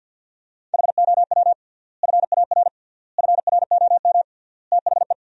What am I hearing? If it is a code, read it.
Morse code, heard as a keyed tone, about 50 wpm, 700 Hz. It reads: HOW FAR FROM THE
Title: 3x Word-Spacing - Sentences from Top 400 Words - 50wpm